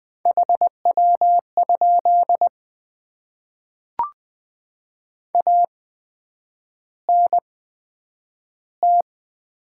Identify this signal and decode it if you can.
Morse code, heard as a keyed tone, about 20 wpm, spaced as for 9 wpm, 700 Hz. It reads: HW? ANT
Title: Top 20 QSO Elements - 8x Character Spacing - 20wpm